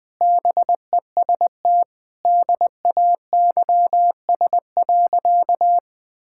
Morse code, heard as a keyed tone, about 20 wpm, 700 Hz. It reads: BEST DAYS.
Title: Sentences with 5-Word Phrases - 20wpm